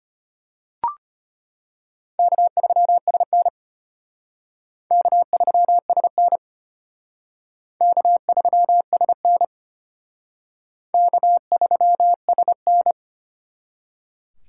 Morse code, heard as a keyed tone, about 32 wpm, 700 Hz. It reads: K3SN K3SN K3SN K3SN